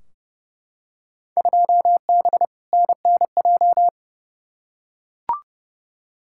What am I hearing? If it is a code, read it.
Morse code, heard as a keyed tone, about 30 wpm, 700 Hz. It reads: 2B NNJ
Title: Field Day Exchanges Encore - 30wpm